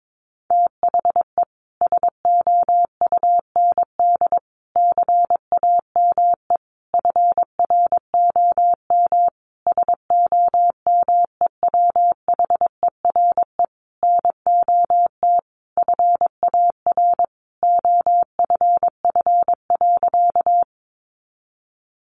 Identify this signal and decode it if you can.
Morse code, heard as a keyed tone, about 22 wpm, 700 Hz. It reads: THE SOUND CAME FROM SOMEWHERE NOT FAR OFF.
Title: Sentences from Top 1400 Words 22wpm